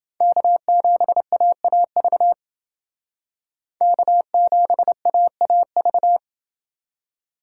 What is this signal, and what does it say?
Morse code, heard as a keyed tone, about 28 wpm, 700 Hz. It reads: K7AAV K7AAV